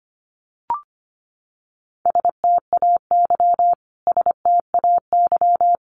Morse code, heard as a keyed tone, about 25 wpm, 700 Hz. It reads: STAY STAY
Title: Top 500 Words Repeated 25wpm